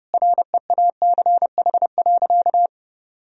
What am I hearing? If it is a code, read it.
Morse code, heard as a keyed tone, about 30 wpm, 700 Hz. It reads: REACH.
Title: Sentences from Top 1000 Words - Review Part 2 - 30wpm